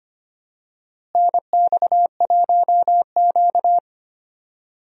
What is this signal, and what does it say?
Morse code, heard as a keyed tone, about 25 wpm, 700 Hz. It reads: NX1Q
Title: USA Call Signs - 4 Characters - Repeated 3x - 25wpm